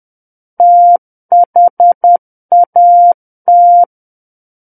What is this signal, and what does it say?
Morse code, heard as a keyed tone, about 10 wpm, 700 Hz. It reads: THAT
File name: Book One / 01-Chapter 1 - 10 WPM